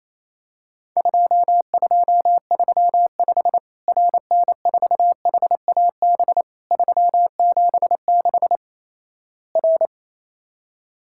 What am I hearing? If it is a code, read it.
Morse code, heard as a keyed tone, about 28 wpm, 700 Hz. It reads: 2235 RN4HAB 376 R
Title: Worked All Europe DX Contest - QTC Exchange - 28wpm